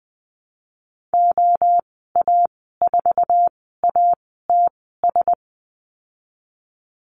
Morse code, heard as a keyed tone, about 20 wpm, spaced as for 15 wpm, 700 Hz. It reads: OA4ATS
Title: 088 Lesson 15g - Number 5 - Call Signs 20wpm